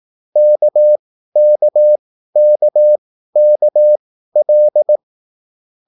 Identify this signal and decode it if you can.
Morse code, heard as a keyed tone, about 18 wpm, spaced as for 13 wpm, 600 Hz. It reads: KKKKL